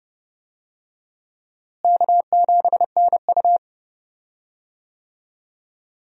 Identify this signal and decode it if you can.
Morse code, heard as a keyed tone, about 32 wpm, 700 Hz. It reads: K7NU
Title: USA Call Signs - 4 Characters - Repeated 3x - 30wpm